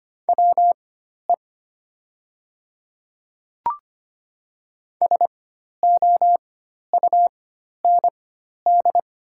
Morse code, heard as a keyed tone, about 25 wpm, spaced as for 12 wpm, 700 Hz. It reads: WE SOUND
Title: 4x Character Spacing - Top 100 Words - 25wpm